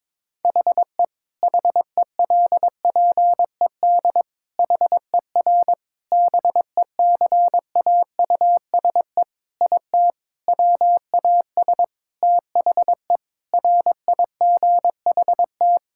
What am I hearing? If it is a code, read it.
Morse code, heard as a keyed tone, about 22 wpm, 700 Hz. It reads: HE HELPED HER BECAUSE IT WAS THE RIGHT
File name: practice / 22wpm / Sentences with 8-Word Phrases - 22wpm